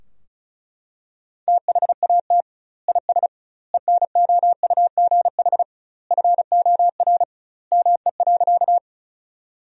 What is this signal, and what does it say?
Morse code, heard as a keyed tone, about 36 wpm, 700 Hz. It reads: THAT IS ENOUGH FOR ME.
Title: 2x Word-Spacing - Sentences from Top 400 Words - 35wpm